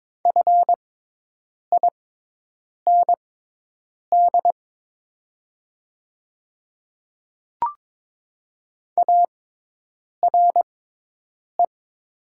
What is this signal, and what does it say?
Morse code, heard as a keyed tone, about 22 wpm, spaced as for 8 wpm, 700 Hz. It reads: FIND ARE
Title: 6x Character Spacing - Top 100 Words - 22wpm